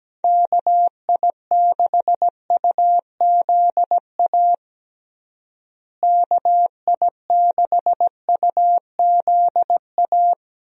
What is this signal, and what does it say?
Morse code, heard as a keyed tone, about 17 wpm, 700 Hz. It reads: KI6UZA KI6UZA
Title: USA Call Signs - 6 Characters - Repeated 2x - 17wpm